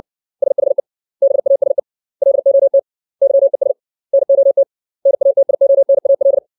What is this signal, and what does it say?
Morse code, heard as a keyed tone, about 60 wpm, 550 Hz. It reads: SHE BASE LOT YES NOT NATIONAL
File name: 632 words / 632 Words 60WPM